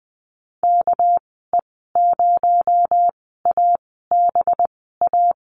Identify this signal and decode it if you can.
Morse code, heard as a keyed tone, about 20 wpm, spaced as for 15 wpm, 700 Hz. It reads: KE0ABA
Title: 219 Lesson 35g - Number 0 - Call Signs 20wpm